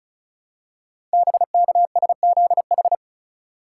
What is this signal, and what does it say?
Morse code, heard as a keyed tone, about 36 wpm, 700 Hz. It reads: BKSZH
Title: Code Groups - 5 Letters - 35wpm